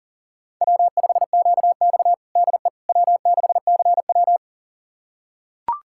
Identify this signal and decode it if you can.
Morse code, heard as a keyed tone, about 40 wpm, 700 Hz. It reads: W5QX DE W6CW
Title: USA Callsign DE Callsign - 4 Characters - 40wpm